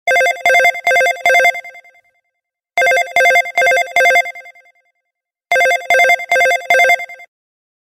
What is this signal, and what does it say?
There is no code.